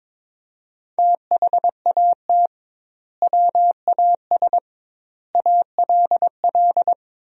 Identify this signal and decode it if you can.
Morse code, heard as a keyed tone, about 22 wpm, 700 Hz. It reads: THAT WAS ALL
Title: 2x Word-Spacing - Sentences from Top 100 Words - 22wpm